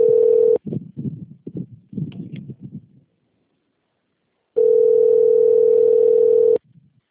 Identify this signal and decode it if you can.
Ringback tone (North America), 440 + 480 Hz.
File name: out- - - - - .5483